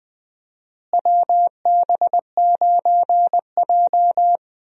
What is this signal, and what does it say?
Morse code, heard as a keyed tone, about 20 wpm, 700 Hz. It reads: WB9J
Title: USA Call Signs - 4 Characters - Repeated 3x - 20wpm